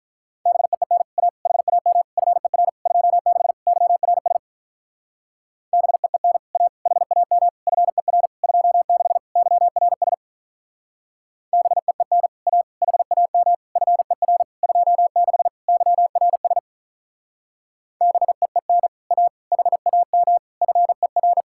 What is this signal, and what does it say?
Morse code, heard as a keyed tone, about 43 wpm, 700 Hz. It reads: BEEN A HAM FER 26 YRS BEEN A HAM FER 26 YRS BEEN A HAM FER 26 YRS BEEN A HAM FER
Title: Speed-Racing Common QSO Phrases Encore - 35wpm